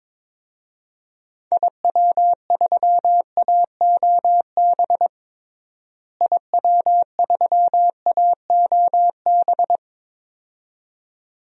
Morse code, heard as a keyed tone, about 22 wpm, 700 Hz. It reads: IW3AOB IW3AOB
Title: DXCC-100 Call Signs - 6 Characters - Repeated 2x - 22wpm